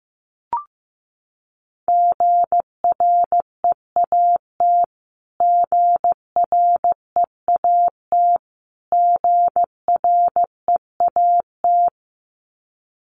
Morse code, heard as a keyed tone, about 15 wpm, 700 Hz. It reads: GREAT GREAT GREAT
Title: Top 500 Words Repeated 15wpm